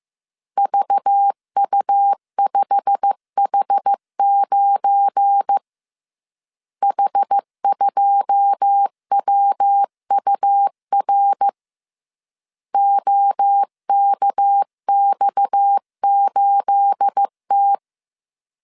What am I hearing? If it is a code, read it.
Morse code, heard as a keyed tone, about 15 wpm, 800 Hz. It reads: VU5H9 H2WUR OKX8T
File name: CW / DET / 1 9 1500C S C 14wpm 3245